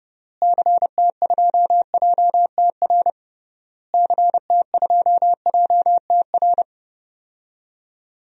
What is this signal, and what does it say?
Morse code, heard as a keyed tone, about 30 wpm, 700 Hz. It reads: CT2JTR CT2JTR
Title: DXCC-100 Call Signs - 6 Characters - Repeated 3x - 30wpm